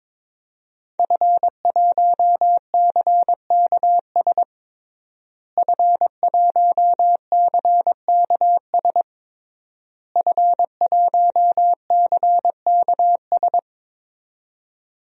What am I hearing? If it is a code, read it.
Morse code, heard as a keyed tone, about 22 wpm, 700 Hz. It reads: F1CKS F1CKS F1CKS
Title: DXCC-100 Call Signs - 5 Characters - Repeated 3x - 22wpm